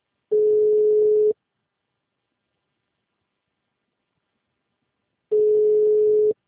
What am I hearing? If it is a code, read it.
Ringback tone (Europe, 425 Hz).
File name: out- -2001- - - .167